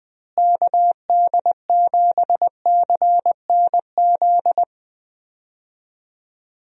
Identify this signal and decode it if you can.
Morse code, heard as a keyed tone, about 20 wpm, 700 Hz. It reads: KD7CNZ
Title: USA Call Signs - 6 Characters - Repeated 2x - 20wpm